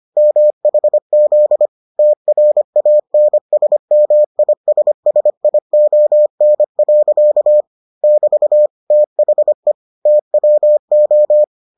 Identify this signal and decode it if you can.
Morse code, heard as a keyed tone, about 25 wpm, 600 Hz. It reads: MHZ TRANSMISSION. <BT> THE TWO